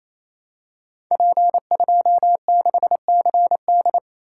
Morse code, heard as a keyed tone, about 27 wpm, 700 Hz. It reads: P26CD
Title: Code Groups - 5 Letter:Numbers - 28wpm